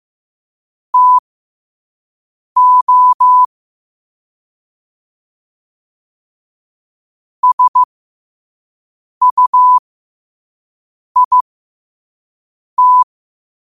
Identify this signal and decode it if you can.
Morse code, heard as a keyed tone, about 15 wpm, spaced as for 5 wpm, 1000 Hz. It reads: TO SUIT
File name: qotdfiles / 5wpm